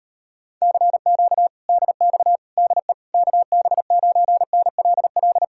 Morse code, heard as a keyed tone, about 37 wpm, 700 Hz. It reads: CQ DX DE KB9NLL